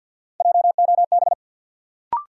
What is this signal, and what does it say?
Morse code, heard as a keyed tone, about 52 wpm, 700 Hz. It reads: JOB
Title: Sentences from Top 600 Words - New Word Drill - 50wpm